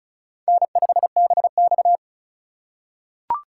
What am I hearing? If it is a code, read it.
Morse code, heard as a keyed tone, about 36 wpm, 700 Hz. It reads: N5BX
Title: USA Call Signs - 4 Characters - 2x Character Spacing - 35wpm